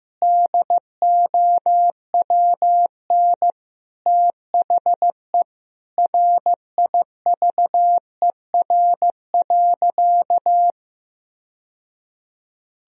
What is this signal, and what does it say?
Morse code, heard as a keyed tone, about 15 wpm, 700 Hz. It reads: DOWN THE RIVER.
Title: Sentences from Top 1000 Words - Review Part 1 - 15wpm